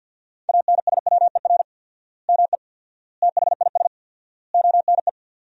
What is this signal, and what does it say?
Morse code, heard as a keyed tone, about 50 wpm, 700 Hz. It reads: ANSWER ME THIS ONE
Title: 4x Word-Spacing - Sentences from Top 400 Words - 50wpm